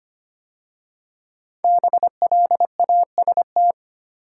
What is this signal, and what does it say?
Morse code, heard as a keyed tone, about 25 wpm, 700 Hz. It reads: BLAST
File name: 5 Letter Words - First 3 Letters Given - 25wpm